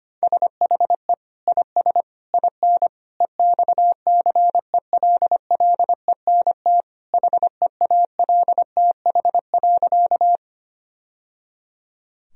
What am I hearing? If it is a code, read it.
Morse code, heard as a keyed tone, about 25 wpm, 700 Hz. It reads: SHE IS IN EXCELLENT HEALTH.
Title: Sentences from Top 1000 Words - Review Part 3 - 25wpm